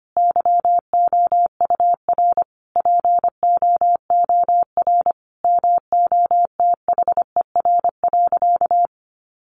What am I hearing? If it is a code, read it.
Morse code, heard as a keyed tone, about 25 wpm, 700 Hz. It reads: YOUR POOR MOTHER.
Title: Sentences from Top 1000 Words 25wpm